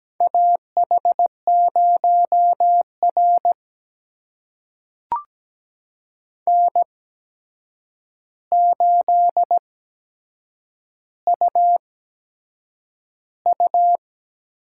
Morse code, heard as a keyed tone, about 17 wpm, spaced as for 7 wpm, 700 Hz. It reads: AH0R N8UU
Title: USA Call Signs - 4 Characters - 8x Character Spacing - 17wpm